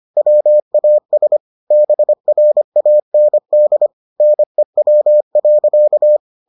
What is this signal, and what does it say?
Morse code, heard as a keyed tone, about 25 wpm, 600 Hz. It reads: WAS BRAND NEW.